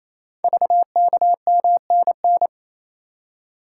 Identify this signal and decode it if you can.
Morse code, heard as a keyed tone, about 29 wpm, 690 Hz. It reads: VKMNN
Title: Code Groups - 5 Letters - 28wpm